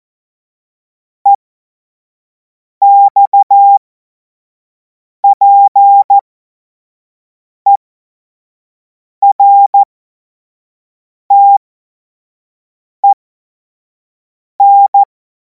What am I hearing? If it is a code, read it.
Morse code, heard as a keyed tone, about 14 wpm, spaced as for 5 wpm, 800 Hz. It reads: EXPERTEN
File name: Morsekurs / Witze / Witz 24. Februar 2026 25 F70